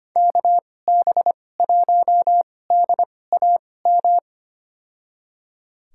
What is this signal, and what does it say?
Morse code, heard as a keyed tone, about 25 wpm, spaced as for 18 wpm, 700 Hz. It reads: KB1DAM
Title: USA Call Signs - 6 Characters - 2x Character Spacing - 25wpm